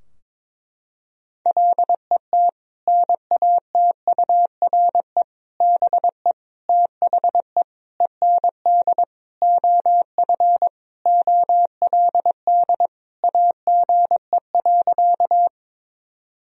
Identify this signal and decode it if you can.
Morse code, heard as a keyed tone, about 22 wpm, 700 Hz. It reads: LET NATURE BE THE END OF OLD AGE.
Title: Sentences from Top 1000 Words - Review Part 3 - 22wpm